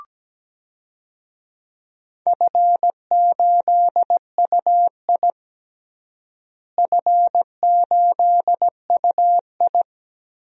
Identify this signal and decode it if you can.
Morse code, heard as a keyed tone, about 17 wpm, 700 Hz. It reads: F8UI F8UI
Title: DXCC-100 Call Signs - 4 Characters - Repeated 2x - 17wpm